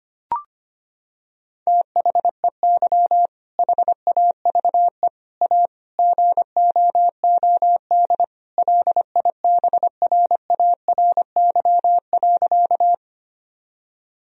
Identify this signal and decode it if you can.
Morse code, heard as a keyed tone, about 25 wpm, 700 Hz. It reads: THEY HAVE A GOOD LIBRARY.
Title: Sentences from Top 1000 Words - Review Part 4 - 25wpm